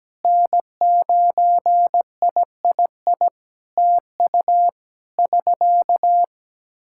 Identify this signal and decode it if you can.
Morse code, heard as a keyed tone, about 17 wpm, 700 Hz. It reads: N9III TU <SK>
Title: Practice QSO - Exchange 4 - Ending - 17wpm